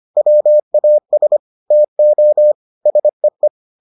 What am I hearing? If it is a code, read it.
Morse code, heard as a keyed tone, about 25 wpm, 600 Hz. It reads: WAS TO SEE